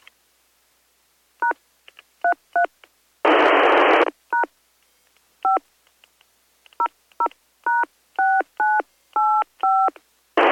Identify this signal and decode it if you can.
DTMF, heard as telephone touch tones, tones of about 130 ms. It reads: #33 # 5 00#6985